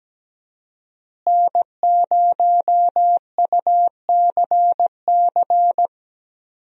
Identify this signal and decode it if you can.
Morse code, heard as a keyed tone, about 17 wpm, 700 Hz. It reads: N0UCC
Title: USA Call Signs - 5 Characters - Repeated 2x - 17wpm